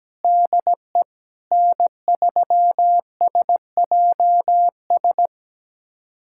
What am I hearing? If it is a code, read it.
Morse code, heard as a keyed tone, about 17 wpm, 700 Hz. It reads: DE N3SJS
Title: USA Callsign DE Callsign 17wpm